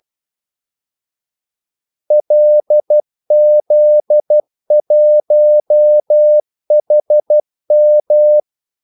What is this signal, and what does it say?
Morse code, heard as a keyed tone, about 12 wpm, 600 Hz. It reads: LZ1HM
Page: 2 ekstra ordmellemrum mellem kaldesignalerne